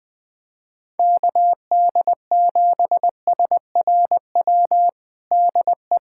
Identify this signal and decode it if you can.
Morse code, heard as a keyed tone, about 20 wpm, 700 Hz. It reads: KD7SRW DE
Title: USA Callsign DE Callsign - 6 Characters - 20wpm